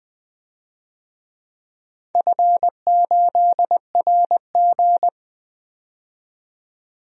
Morse code, heard as a keyed tone, about 21 wpm, 700 Hz. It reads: F8RG
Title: DXCC-100 Call Signs - 4 Characters - 20wpm